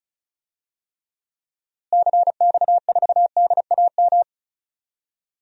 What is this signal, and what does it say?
Morse code, heard as a keyed tone, about 34 wpm, 700 Hz. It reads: CX4DAM
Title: DXCC-100 Call Signs - 6 Characters - Repeated 2x - 35wpm